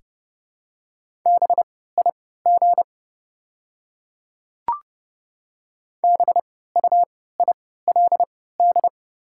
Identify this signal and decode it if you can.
Morse code, heard as a keyed tone, about 31 wpm, spaced as for 17 wpm, 700 Hz. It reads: BIG BUILD
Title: 3x Character Spacing - Top 500 Words - 30wpm